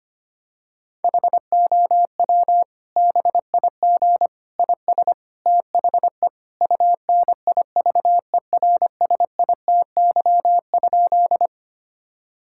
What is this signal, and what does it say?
Morse code, heard as a keyed tone, about 25 wpm, 700 Hz. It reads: HOW BIG IS THE UNIVERSITY?
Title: Sentences from Top 1000 Words - Review Part 5 - 25wpm